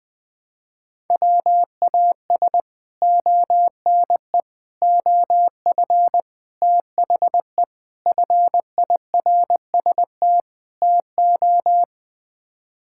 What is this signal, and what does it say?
Morse code, heard as a keyed tone, about 21 wpm, 700 Hz. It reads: WAS ONE OF THE FIRST TO
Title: Sets of 6 Words - 20wpm